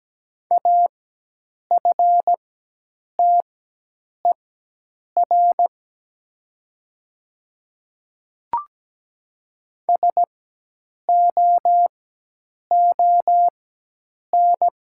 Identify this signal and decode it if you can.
Morse code, heard as a keyed tone, about 17 wpm, spaced as for 8 wpm, 700 Hz. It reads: AFTER SOON
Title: 4x Character Spacing - Top 500 Words - 17wpm